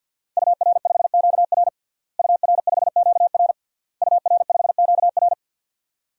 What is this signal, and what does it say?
Morse code, heard as a keyed tone, about 48 wpm, 700 Hz. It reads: UR5QR UR5QR UR5QR
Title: DXCC-100 Call Signs - 5 Characters - Repeated 3x - 50wpm